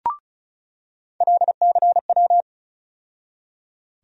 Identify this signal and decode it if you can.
Morse code, heard as a keyed tone, about 34 wpm, 700 Hz. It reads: LCW
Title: USA Call Sign Postfixes - 3 Letters 35wpm